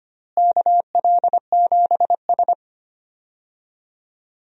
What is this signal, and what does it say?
Morse code, heard as a keyed tone, about 25 wpm, 700 Hz. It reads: KL7S
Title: DXCC-100 Call Signs - 4 Characters - Repeated 3x - 25wpm